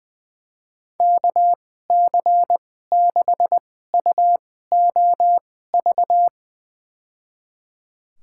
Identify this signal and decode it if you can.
Morse code, heard as a keyed tone, about 20 wpm, spaced as for 14 wpm, 700 Hz. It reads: KC6UOV
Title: USA Call Signs - 6 Characters - 2x Character Spacing - 20wpm